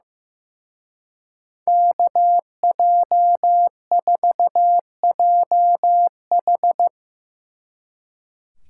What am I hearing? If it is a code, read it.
Morse code, heard as a keyed tone, about 16 wpm, 700 Hz. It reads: KJ4JH
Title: Speed Racing - USA 2x2 Call Signs 15wpm